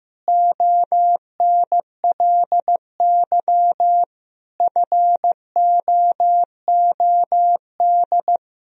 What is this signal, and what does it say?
Morse code, heard as a keyed tone, about 15 wpm, 700 Hz. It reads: ONLY FOOD
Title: Sentences from Top 300 Words and Abbreviations 15wpm